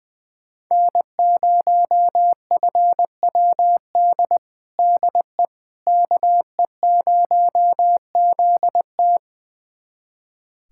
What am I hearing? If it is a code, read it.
Morse code, heard as a keyed tone, about 20 wpm, 700 Hz. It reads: N0FWD DE KE0ZT
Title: USA Callsign DE Callsign - 5 Characters - 20wpm